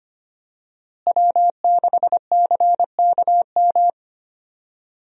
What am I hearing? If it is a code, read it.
Morse code, heard as a keyed tone, about 25 wpm, 700 Hz. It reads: W6CKM
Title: USA Call Signs - 5 Characters - Repeated 3x - 25wpm